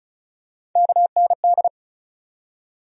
Morse code, heard as a keyed tone, about 36 wpm, 700 Hz. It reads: KND
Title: USA Call Sign Postfixes - 3 Letters 35wpm